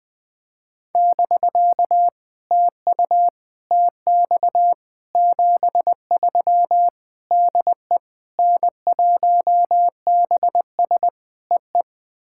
Morse code, heard as a keyed tone, about 20 wpm, 700 Hz. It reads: <BK> TU TX 73 DE N1BS EE
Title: POTA QSO Practice - 20wpm